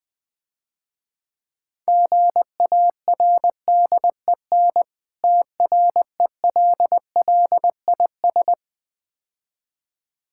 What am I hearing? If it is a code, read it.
Morse code, heard as a keyed tone, about 20 wpm, 700 Hz. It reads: GARDEN TRELLIS
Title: Difficult Words - Speed-Racing with Hints - 20wpm